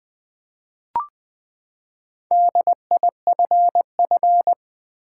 Morse code, heard as a keyed tone, about 20 wpm, 700 Hz. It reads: DIFF
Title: Build A Word - 4 Letters At A Time - 10-Letter Words - 20wpm